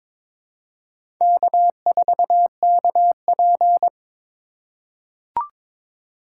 Morse code, heard as a keyed tone, about 21 wpm, 700 Hz. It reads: K4KP
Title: USA Call Signs - 2x Character Spacing - 22wpm